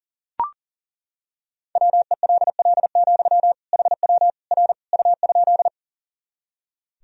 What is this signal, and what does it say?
Morse code, heard as a keyed tone, about 40 wpm, 700 Hz. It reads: WELL, HW R U?
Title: Sentences from Top 200 Words and Abbreviations 40wpm